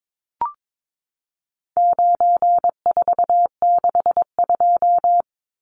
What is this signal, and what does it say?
Morse code, heard as a keyed tone, about 22 wpm, 700 Hz. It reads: 9462
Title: Numbers - 4-Digit 22wpm